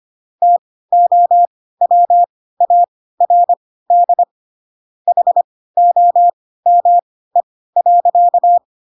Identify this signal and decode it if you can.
Morse code, heard as a keyed tone, about 25 wpm, spaced as for 16 wpm, 700 Hz. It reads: TOWARD HOME.
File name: Pinocchio / Pinocchio ch2 10 25 ik1hge